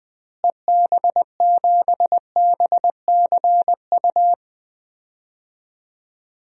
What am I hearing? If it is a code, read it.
Morse code, heard as a keyed tone, about 21 wpm, 700 Hz. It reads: EB7BCU
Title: DXCC-100 Call Signs - 6 Characters - Repeated 3x - 20wpm